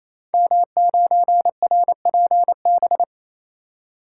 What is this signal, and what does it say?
Morse code, heard as a keyed tone, about 27 wpm, 700 Hz. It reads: M9RPB
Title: Code Groups - 5 Letter:Numbers - 28wpm